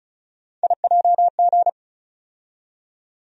Morse code, heard as a keyed tone, about 36 wpm, 700 Hz. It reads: IJG
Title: USA Call Sign Postfixes - 3 Letters 35wpm